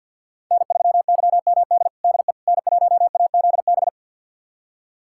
Morse code, heard as a keyed tone, about 48 wpm, 700 Hz. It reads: N3YMD DE N1AZB
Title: USA Callsign DE Callsign - 5 Characters - 50wpm